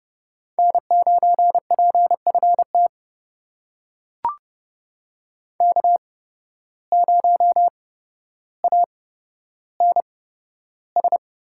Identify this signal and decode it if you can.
Morse code, heard as a keyed tone, about 30 wpm, spaced as for 13 wpm, 700 Hz. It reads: N9PFT K0ANS
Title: USA Call Signs - 5 Characters - 8x Character Spacing - 30wpm